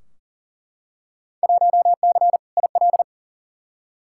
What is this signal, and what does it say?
Morse code, heard as a keyed tone, about 40 wpm, 700 Hz. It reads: 1C IL
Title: Field Day Exchanges - 40wpm